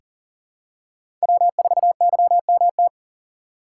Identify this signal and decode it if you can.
Morse code, heard as a keyed tone, about 40 wpm, 700 Hz. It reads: W4YMT
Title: USA Call Signs - 5 Characters - 3x Character Spacing - 40wpm